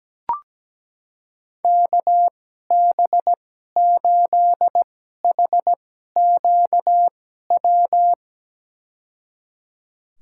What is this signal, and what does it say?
Morse code, heard as a keyed tone, about 17 wpm, spaced as for 12 wpm, 700 Hz. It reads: KB8HQW
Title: USA Call Signs - 6 Characters - 2x Character Spacing - 17wpm